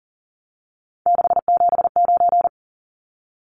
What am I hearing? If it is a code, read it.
Morse code, heard as a keyed tone, about 40 wpm, 700 Hz. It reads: 679
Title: Numbers - 3-Digit 40wpm